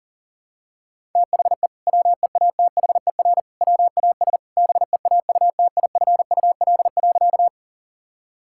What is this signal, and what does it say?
Morse code, heard as a keyed tone, about 40 wpm, 700 Hz. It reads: THE WEATHER WAS BEAUTIFUL.
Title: Sentences from Top 1000 Words - Review Part 5 - 40wpm